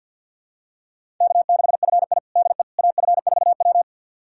Morse code, heard as a keyed tone, about 53 wpm, 700 Hz. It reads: K6FI DE AF4W
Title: USA Callsign DE Callsign - 4 Characters - 50wpm